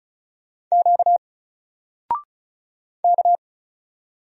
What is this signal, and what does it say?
Morse code, heard as a keyed tone, about 36 wpm, 700 Hz. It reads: Q K
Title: Single Letter-Number Alt 35wpm